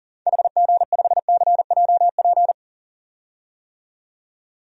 Morse code, heard as a keyed tone, about 40 wpm, 700 Hz. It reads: HG5CJP
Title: DXCC-100 Call Signs - 6 Characters - 40wpm